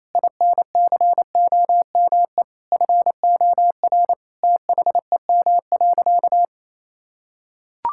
Morse code, heard as a keyed tone, about 28 wpm, 700 Hz. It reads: INCOME FOR THEM.
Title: Sentences from Top 1000 Words - Review Part 5 - 28wpm